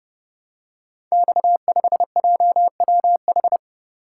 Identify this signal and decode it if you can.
Morse code, heard as a keyed tone, about 30 wpm, 700 Hz. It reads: X5JWH
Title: Code Groups - 5 Letter:Numbers - 30wpm